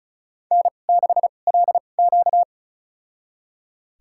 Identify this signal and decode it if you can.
Morse code, heard as a keyed tone, about 36 wpm, spaced as for 25 wpm, 700 Hz. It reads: N6LQ
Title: USA Call Signs - 4 Characters - 2x Character Spacing - 35wpm